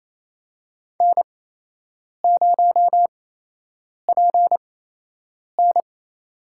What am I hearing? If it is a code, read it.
Morse code, heard as a keyed tone, about 28 wpm, spaced as for 8 wpm, 700 Hz. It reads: N0PN
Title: USA Call Signs - 4 Characters - 8x Character Spacing - 28wpm